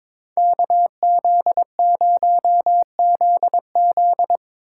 Morse code, heard as a keyed tone, about 24 wpm, 700 Hz. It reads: KZ0ZZ
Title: Speed Racing - USA 2x2 Call Signs 22wpm